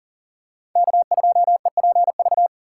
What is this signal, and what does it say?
Morse code, heard as a keyed tone, about 40 wpm, 690 Hz. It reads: K2EPV
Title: USA Call Signs - 5 Characters - 6x Character Spacing - 40wpm